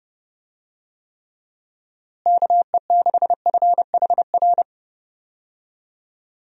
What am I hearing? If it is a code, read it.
Morse code, heard as a keyed tone, about 30 wpm, 700 Hz. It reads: KE6FHR
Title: USA Call Signs - 6 Characters - Repeated 2x - 30wpm